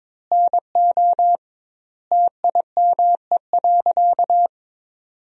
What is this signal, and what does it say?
Morse code, heard as a keyed tone, about 22 wpm, 700 Hz. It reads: NO TIME.
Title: 2x Word-Spacing - Sentences from Top 100 Words - 22wpm